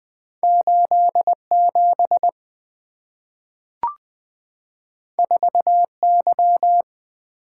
Morse code, heard as a keyed tone, about 20 wpm, 700 Hz. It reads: 8 7 4 Y
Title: Two Letter-Number Combinations - 20wpm